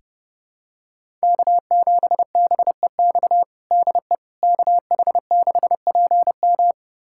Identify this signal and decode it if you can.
Morse code, heard as a keyed tone, about 30 wpm, 700 Hz. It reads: K7BEX DE KH6PM
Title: USA Callsign DE Callsign - 5 Characters - 30wpm